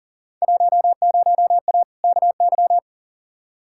Morse code, heard as a keyed tone, about 38 wpm, 700 Hz. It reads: 10A KY
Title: Field Day Exchanges Encore - 40wpm